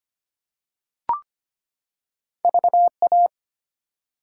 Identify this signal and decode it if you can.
Morse code, heard as a keyed tone, about 25 wpm, 700 Hz. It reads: VA
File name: practice / 25wpm / ARRL Sections v2 - 25wpm